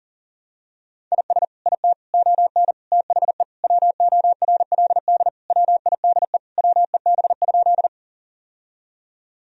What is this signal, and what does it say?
Morse code, heard as a keyed tone, about 40 wpm, 700 Hz. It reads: IS IT ON THE WORLD WIDE WEB?
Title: Sentences from Top 1000 Words - Review Part 4 - 40wpm